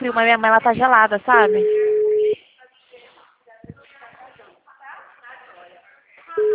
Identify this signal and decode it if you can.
Ringback tone (Europe, 425 Hz), with speech.
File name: out- -2045- - - .6167